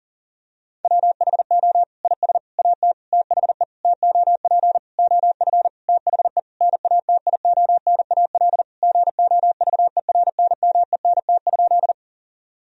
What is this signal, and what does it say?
Morse code, heard as a keyed tone, about 37 wpm, 700 Hz. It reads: WHO IS AT THE TOP OF THE NATIONAL GOVERNMENT?
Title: Sentences from Top 1000 Words - Review Part 4 - 40wpm